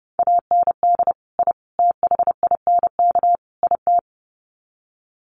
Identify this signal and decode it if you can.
Morse code, heard as a keyed tone, about 30 wpm, 700 Hz. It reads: AND I THINK IT
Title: Sets of 4 Words 30wpm